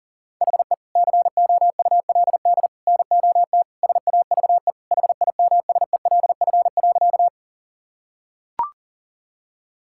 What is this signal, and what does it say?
Morse code, heard as a keyed tone, about 40 wpm, 700 Hz. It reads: HE COULD NOT SAVE HIMSELF.
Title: Sentences from Top 1600 Words 40wpm